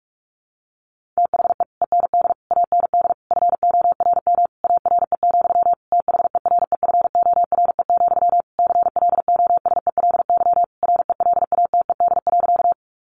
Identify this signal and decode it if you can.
Morse code, heard as a keyed tone, about 44 wpm, 700 Hz. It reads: THE END AND FORM ARE, THEREFORE, CLOSELY RELATED.
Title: Sentences from Top 1400 Words 45wpm